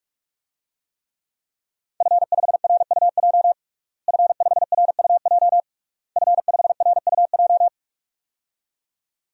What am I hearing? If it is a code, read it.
Morse code, heard as a keyed tone, about 46 wpm, 700 Hz. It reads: F5RUJ F5RUJ F5RUJ
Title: DXCC-100 Call Signs - 5 Characters - Repeated 3x - 45wpm